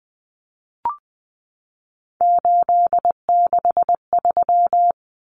Morse code, heard as a keyed tone, about 20 wpm, 700 Hz. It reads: 863
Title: Common Cut Numbers - 3-Digit 20wpm